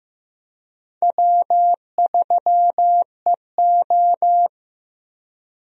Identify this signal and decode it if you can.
Morse code, heard as a keyed tone, about 16 wpm, 700 Hz. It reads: W3EO
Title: USA Call Signs - 4 Characters - Repeated 3x - 15wpm